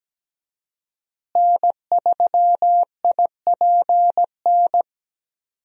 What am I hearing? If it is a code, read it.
Morse code, heard as a keyed tone, about 17 wpm, 700 Hz. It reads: N3IPN
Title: USA Call Signs - 5 Characters - Repeated 3x - 17wpm